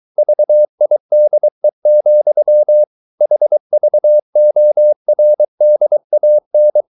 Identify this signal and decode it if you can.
Morse code, heard as a keyed tone, about 23 wpm, 600 Hz. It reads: VIDE, HVORDAN
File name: Ferie QRS 23wpm